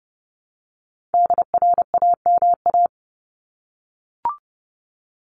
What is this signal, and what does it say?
Morse code, heard as a keyed tone, about 30 wpm, 700 Hz. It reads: DRAMA
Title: Top 1251-1500 Words 30wpm